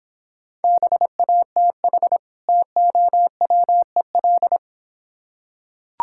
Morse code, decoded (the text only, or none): BATH TOWEL